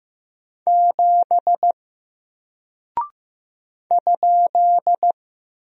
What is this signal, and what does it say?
Morse code, heard as a keyed tone, about 15 wpm, 700 Hz. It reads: 7 ?
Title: Single Character (Letter:Number:Punctuation:Prosign) 15wpm